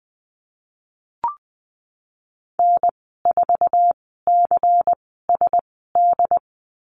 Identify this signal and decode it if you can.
Morse code, heard as a keyed tone, about 19 wpm, spaced as for 14 wpm, 700 Hz. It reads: N4CSD
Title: 128 Lesson 21g - Number 6 - Call Signs 20wpm